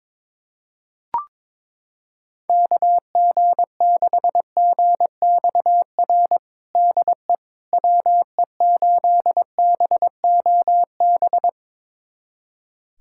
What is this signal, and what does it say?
Morse code, heard as a keyed tone, about 22 wpm, 700 Hz. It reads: KG6GXR DE WE8BOB
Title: USA Callsign DE Callsign 22wpm